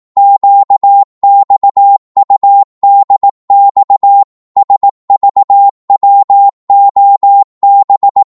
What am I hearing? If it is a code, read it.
Morse code, heard as a keyed tone, about 18 wpm, 800 Hz. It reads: QXUDX SVWOB
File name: Morse / ltr-18wpm